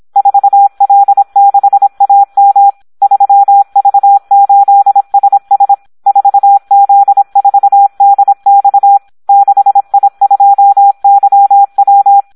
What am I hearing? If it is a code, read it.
Morse code, heard as a keyed tone, about 26 wpm, 800 Hz. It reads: 4L6AM 3V8SS 4Z4DX 6I2YW
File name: Audio practica CW / SD 1 26